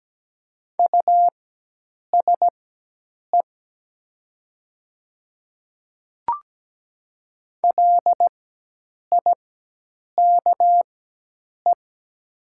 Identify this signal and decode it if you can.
Morse code, heard as a keyed tone, about 17 wpm, spaced as for 8 wpm, 700 Hz. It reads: USE LIKE
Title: 4x Character Spacing - Top 100 Words - 17wpm